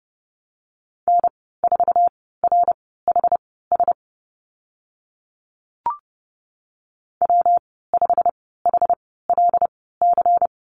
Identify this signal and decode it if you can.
Morse code, heard as a keyed tone, about 30 wpm, spaced as for 17 wpm, 700 Hz. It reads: N4RHS W5HLC
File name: Lesson 19g - Letter w - Call Signs 30wpm